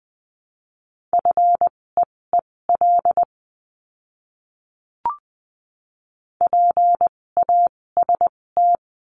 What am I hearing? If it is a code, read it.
Morse code, heard as a keyed tone, about 20 wpm, spaced as for 16 wpm, 700 Hz. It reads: FEEL PAST
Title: 256 Lesson 42a - Words 12-20wpm